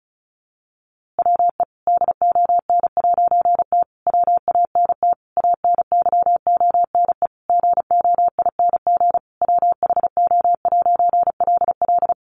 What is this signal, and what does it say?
Morse code, heard as a keyed tone, about 34 wpm, 700 Hz. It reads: WE DON'T WANT ANYONE GOING WHO'LL
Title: A Sound Of Thunder 35wpm